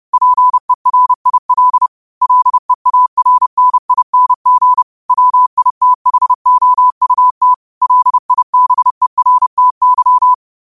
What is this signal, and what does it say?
Morse code, heard as a keyed tone, about 30 wpm, 1000 Hz. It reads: PERIL LEARNING WITHOUT LIBERTY